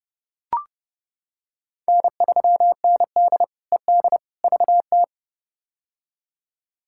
Morse code, heard as a keyed tone, about 32 wpm, 700 Hz. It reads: N3ND ED VT
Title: K1USN SST QSO Practice - 30wpm